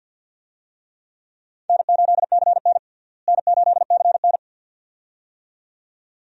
Morse code, heard as a keyed tone, about 46 wpm, 700 Hz. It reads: N8CN N8CN
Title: USA Call Signs - 4 Characters - Repeated 2x - 50wpm